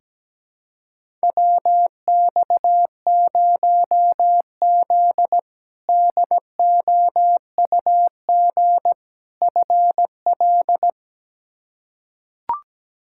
Morse code, heard as a keyed tone, about 17 wpm, 700 Hz. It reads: WX0Z DOUG FL
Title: K1USN SST QSO Practice - 17wpm